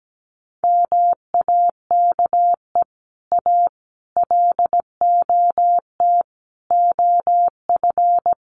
Morse code, heard as a keyed tone, about 17 wpm, 700 Hz. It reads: MAKE A LOT OF
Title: Sets of 5 Words Encore2 17wpm